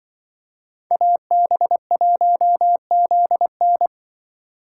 Morse code, heard as a keyed tone, about 25 wpm, 700 Hz. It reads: AB1ZN
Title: Speed Racing - USA 2x2 Call Signs 20wpm